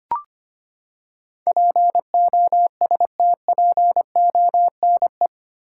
Morse code, heard as a keyed tone, about 25 wpm, 700 Hz. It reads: POSTPONE
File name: Eight Letter Words - 25wpm